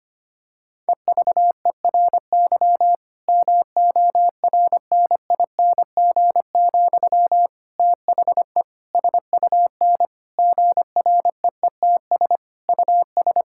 Morse code, heard as a keyed tone, about 25 wpm, 700 Hz. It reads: EVERY MORNING, THE SUN GREETS US
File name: practice / 25wpm / Difficult Sentences with Hints - 25wpm